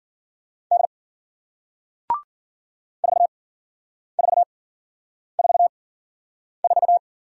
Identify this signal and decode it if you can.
Morse code, heard as a keyed tone, about 54 wpm, spaced as for 9 wpm, 700 Hz. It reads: D 4444